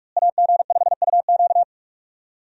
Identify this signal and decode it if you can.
Morse code, heard as a keyed tone, about 42 wpm, 700 Hz. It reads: AG5UQ
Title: USA Call Signs - 5 Characters - Repeated 3x - 45wpm